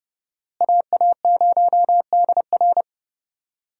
Morse code, heard as a keyed tone, about 32 wpm, 700 Hz. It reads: AA0DR
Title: USA Call Signs - 5 Characters - Repeated 2x - 30wpm